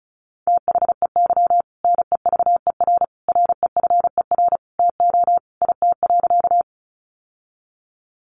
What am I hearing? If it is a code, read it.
Morse code, heard as a keyed tone, about 34 wpm, 700 Hz. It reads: THEY NEVER REFER TO IT.
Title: Sentences from Top 1000 Words 35wpm